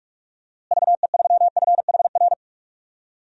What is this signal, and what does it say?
Morse code, heard as a keyed tone, about 46 wpm, 700 Hz. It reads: VE3FHR
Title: DXCC-100 Call Signs - 6 Characters - Repeated 2x - 45wpm